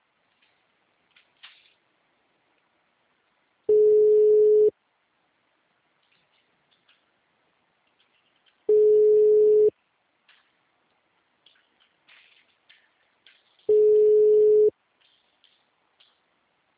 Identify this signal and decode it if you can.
Ringback tone (Europe, 425 Hz).